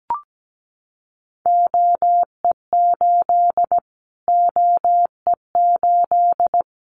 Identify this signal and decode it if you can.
Morse code, heard as a keyed tone, about 17 wpm, 700 Hz. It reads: OE8 OE8
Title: Callsign Prefixes for Top 25 DXCC - Repeated 3x - 17wpm